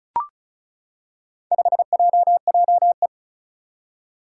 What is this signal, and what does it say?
Morse code, heard as a keyed tone, about 36 wpm, 700 Hz. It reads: 5JJE
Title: Four Challenging Letter-Number Combinations 35wpm